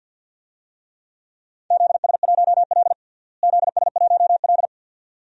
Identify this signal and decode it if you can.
Morse code, heard as a keyed tone, about 52 wpm, 700 Hz. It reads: ZS1L ZS1L
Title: DXCC-100 Call Signs - 4 Characters - Repeated 2x - 50wpm